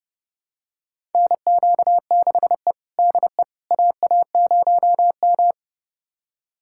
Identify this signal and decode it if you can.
Morse code, heard as a keyed tone, about 31 wpm, 700 Hz. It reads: NQ6E DE AA0M
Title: USA Callsign DE Callsign - 4 Characters - 30wpm